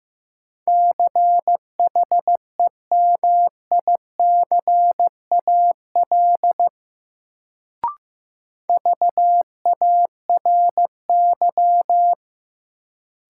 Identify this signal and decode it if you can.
Morse code, heard as a keyed tone, about 16 wpm, 700 Hz. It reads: CHEMICAL VARY
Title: Sentences from Top 1200 Words - New Word Drill - 15wpm